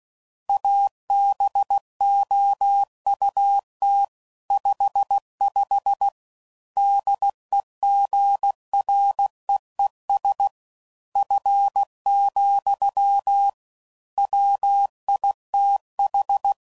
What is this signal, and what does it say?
Morse code, heard as a keyed tone, about 16 wpm, 800 Hz. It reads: ABOUT 55 DEGREES F, WITH
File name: CW Code Groups @15WPM / Weather Sets / Weather 4 24